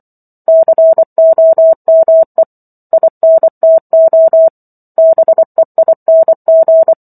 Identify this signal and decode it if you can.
Morse code, heard as a keyed tone, about 24 wpm, 650 Hz. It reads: COME INTO BEING
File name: Book One / 15-Chapter 15 - 24 WPM